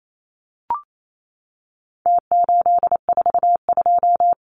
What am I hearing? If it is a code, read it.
Morse code, heard as a keyed tone, about 28 wpm, 700 Hz. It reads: T842
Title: Numbers - Contest Style - 4-Digit 28wpm